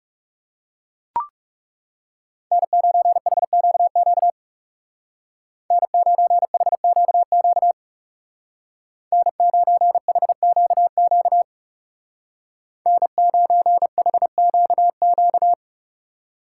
Morse code, heard as a keyed tone, about 39 wpm, 700 Hz. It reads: N9HQQ N9HQQ N9HQQ N9HQQ